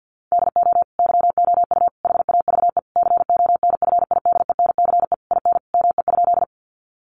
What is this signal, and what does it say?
Morse code, heard as a keyed tone, about 50 wpm, 700 Hz. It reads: DO YOU HAVE CONFIDENCE IN ME?
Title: Sentences from Top 1500 Words 50wpm